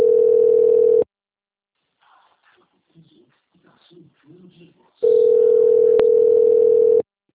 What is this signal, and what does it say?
Ringback tone (North America), 440 + 480 Hz, with speech.
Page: Brasilia - DF